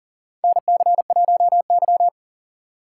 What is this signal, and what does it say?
Morse code, heard as a keyed tone, about 40 wpm, 700 Hz. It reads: NC1Y
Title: USA Call Signs - 4 Characters - 3x Character Spacing - 40wpm